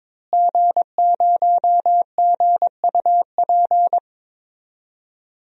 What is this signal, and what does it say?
Morse code, heard as a keyed tone, about 22 wpm, 700 Hz. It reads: G0GUP
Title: DXCC-100 Call Signs - 5 Characters - Repeated 3x - 22wpm